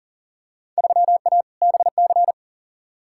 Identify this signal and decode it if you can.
Morse code, heard as a keyed tone, about 40 wpm, 700 Hz. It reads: 3A BC
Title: Field Day Exchanges - 40wpm